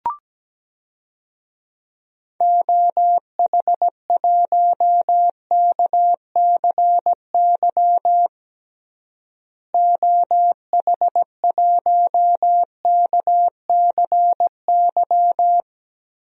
Morse code, heard as a keyed tone, about 17 wpm, 700 Hz. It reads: OH1KCY OH1KCY
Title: DXCC-100 Call Signs - 6 Characters - Repeated 3x - 17wpm